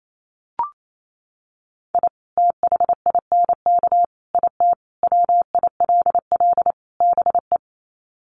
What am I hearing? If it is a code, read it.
Morse code, heard as a keyed tone, about 28 wpm, 700 Hz. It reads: I THINK IT WILL BE
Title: Sets of 5 Words Encore2 28wpm